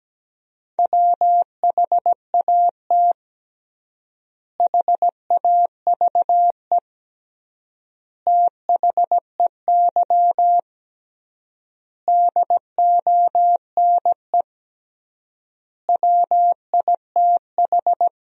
Morse code, heard as a keyed tone, about 17 wpm, 700 Hz. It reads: WHAT HAVE THEY DONE WITH
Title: 3x Word-Spacing - Sentences from Top 500 Words - 17wpm